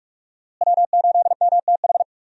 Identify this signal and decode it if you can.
Morse code, heard as a keyed tone, about 46 wpm, 700 Hz. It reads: W8MTH
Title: USA Call Signs - 5 Characters - Repeated 2x - 45wpm